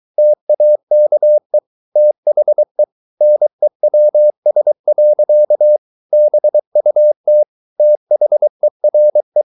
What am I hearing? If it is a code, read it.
Morse code, heard as a keyed tone, about 23 wpm, 600 Hz. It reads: TAKE THE NEWS. BUT THERE